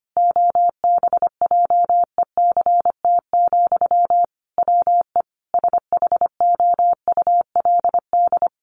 Morse code, heard as a keyed tone, about 25 wpm, 700 Hz. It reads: OBJECT, WE SHOULD
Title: Sentences from Top 800 Words 25wpm